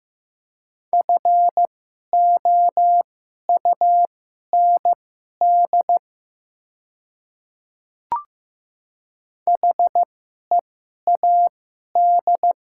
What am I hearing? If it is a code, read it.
Morse code, heard as a keyed tone, about 15 wpm, spaced as for 11 wpm, 700 Hz. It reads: FOUND HEAD
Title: 2x Character Spacing - Top 300 Words - 15wpm